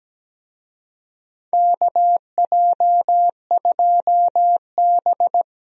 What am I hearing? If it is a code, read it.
Morse code, heard as a keyed tone, about 17 wpm, 700 Hz. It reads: KJ2B
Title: USA Call Signs - 4 Characters - 17wpm